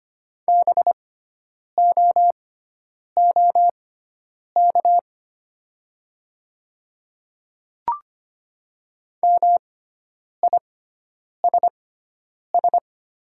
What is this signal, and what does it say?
Morse code, heard as a keyed tone, about 25 wpm, spaced as for 9 wpm, 700 Hz. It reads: BOOK MISS
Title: 6x Character Spacing - Top 500 Words - 25wpm